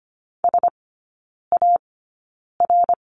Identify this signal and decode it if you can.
Morse code, heard as a keyed tone, about 25 wpm, spaced as for 9 wpm, 700 Hz. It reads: SAR
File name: course / 25:10wpm / 093 Lesson 16e - Letter c - Three Characters 25wpm